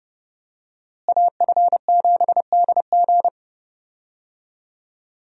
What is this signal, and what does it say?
Morse code, heard as a keyed tone, about 30 wpm, 700 Hz. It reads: AF7DG
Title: USA Call Signs - 5 Characters - Repeated 2x - 30wpm